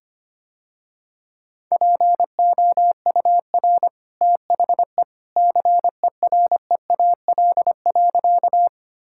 Morse code, heard as a keyed tone, about 25 wpm, 700 Hz. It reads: POUR THE CEREAL.
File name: Easy Sentences with Hints - 25wpm